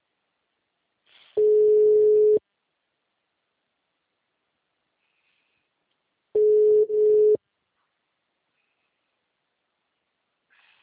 Ringback tone (Europe, 425 Hz).